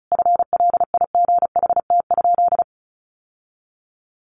Morse code, heard as a keyed tone, about 34 wpm, 700 Hz. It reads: FLIGHT?
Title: Sentences from Top 1000 Words 35wpm